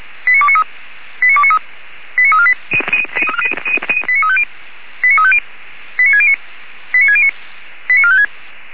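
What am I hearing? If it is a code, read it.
Five-tone selective call (ZVEI), heard as quick runs of five tones, tones of about 70 ms. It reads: 89293 89293 89348 89350 89350 89609 89609 89456
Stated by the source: ZVEI five-tone selective call received by SDR.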